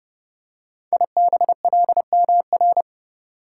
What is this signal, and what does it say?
Morse code, heard as a keyed tone, about 30 wpm, 700 Hz. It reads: IBLMR
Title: Code Groups - 5 Letters - 30wpm